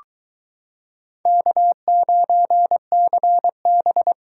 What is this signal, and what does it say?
Morse code, heard as a keyed tone, about 23 wpm, 700 Hz. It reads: K9CB